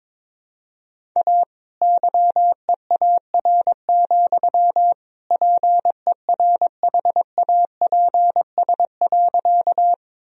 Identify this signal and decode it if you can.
Morse code, heard as a keyed tone, about 22 wpm, 700 Hz. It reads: A YEAR, PERHAPS.
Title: Sentences from Top 1000 Words - Review Part 1 - 22wpm